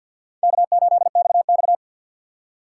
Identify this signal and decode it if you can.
Morse code, heard as a keyed tone, about 46 wpm, 700 Hz. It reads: K8XX
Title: USA Call Signs - 4 Characters - 2x Character Spacing - 50wpm